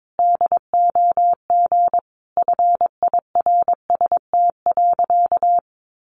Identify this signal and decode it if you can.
Morse code, heard as a keyed tone, about 22 wpm, 700 Hz. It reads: DOG FIRST.
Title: Sentences from Top 400 Words 22wpm